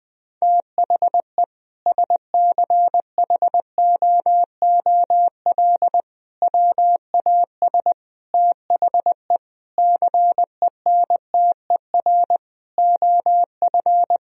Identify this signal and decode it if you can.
Morse code, heard as a keyed tone, about 20 wpm, 700 Hz. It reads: THE SCHOOL WAS THE CENTER OF
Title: Sentences from Top 1000 Words - Review Part 2 - 20wpm